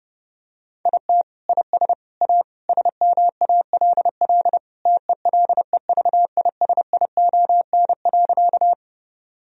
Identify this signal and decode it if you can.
Morse code, heard as a keyed tone, about 30 wpm, 700 Hz. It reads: IT IS A SMALL TELEVISION.
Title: Sentences from Top 1000 Words - Review Part 4 - 30wpm